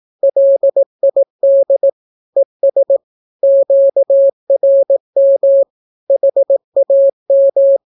Morse code, heard as a keyed tone, about 18 wpm, 550 Hz. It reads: LID ES QRM HAM
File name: naukapodstaw / skroty / skroty